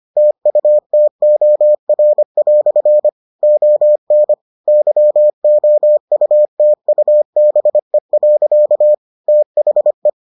Morse code, heard as a keyed tone, about 25 wpm, 600 Hz. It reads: TUTOR" ON YOUTUBE. THE